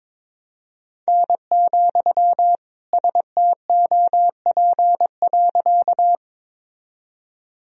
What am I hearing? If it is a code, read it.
Morse code, heard as a keyed tone, about 22 wpm, 700 Hz. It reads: N, STOP.
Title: Sentences from Top 300 Words and Abbreviations 22wpm